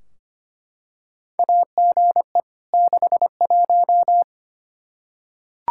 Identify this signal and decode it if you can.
Morse code, heard as a keyed tone, about 25 wpm, 700 Hz. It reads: AGE 61
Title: K1USN SST QSO Practice 25wpm